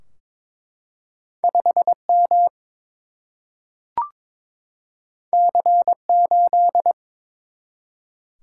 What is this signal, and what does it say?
Morse code, heard as a keyed tone, about 22 wpm, 700 Hz. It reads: 5 M C 8
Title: Single-Letter Single-Number 22wpm